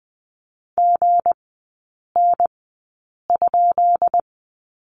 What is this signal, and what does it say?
Morse code, heard as a keyed tone, about 20 wpm, spaced as for 8 wpm, 700 Hz. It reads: GN?
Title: 172 Lesson 28e - Number 9 - Three Characters 20wpm